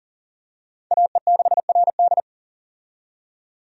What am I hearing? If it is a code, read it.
Morse code, heard as a keyed tone, about 40 wpm, 700 Hz. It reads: AE6RD
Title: USA Call Signs - 5 Characters - 40wpm